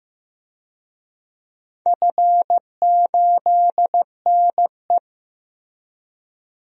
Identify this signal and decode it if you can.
Morse code, heard as a keyed tone, about 15 wpm, 700 Hz. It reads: F8NE
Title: DXCC-100 Call Signs - 4 Characters - Repeated 2x - 15wpm